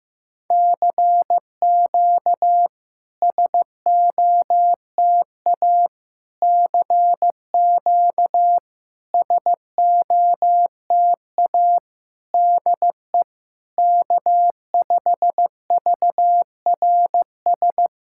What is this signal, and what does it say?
Morse code, heard as a keyed tone, about 15 wpm, 690 Hz. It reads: CQ SOTA CQ SOTA DE K5VRS
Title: SOTA QSO Practice 15wpm